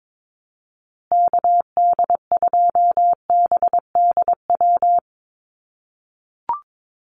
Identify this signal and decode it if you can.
Morse code, heard as a keyed tone, about 22 wpm, 700 Hz. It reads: KD2BDW
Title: USA Call Signs Repeated 3x 22wpm